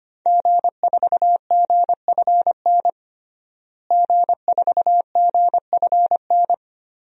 Morse code, heard as a keyed tone, about 25 wpm, 700 Hz. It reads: G4GFN G4GFN
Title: DXCC-100 Call Signs - 5 Characters - Repeated 3x - 25wpm